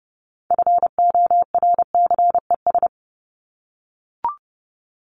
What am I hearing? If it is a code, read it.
Morse code, heard as a keyed tone, about 30 wpm, 700 Hz. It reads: FORCES
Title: ES:S Suffix - Top 100 Words with Base Words 30wpm